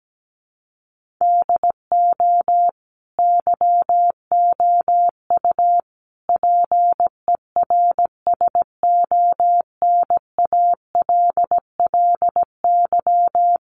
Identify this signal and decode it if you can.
Morse code, heard as a keyed tone, about 17 wpm, 700 Hz. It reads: DO YOU PERSONALLY
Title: Sentences from Top 1400 Words 17wpm